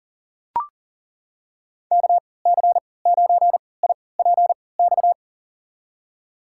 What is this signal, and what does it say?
Morse code, heard as a keyed tone, about 40 wpm, spaced as for 23 wpm, 700 Hz. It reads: KC9IPX
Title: USA Call Signs - 6 Characters - 3x Character Spacing - 40wpm